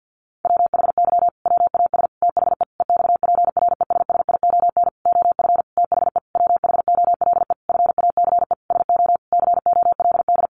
Morse code, heard as a keyed tone, about 50 wpm, 700 Hz. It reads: WHY WAS THE EXPRESSION OF THE WHOLE FACE SO COLD
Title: Sentences from Top 1400 Words 50wpm